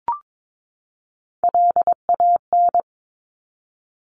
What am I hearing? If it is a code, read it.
Morse code, heard as a keyed tone, about 22 wpm, 700 Hz. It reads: LAN
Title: Top Three Letter Combinations in Words 22wpm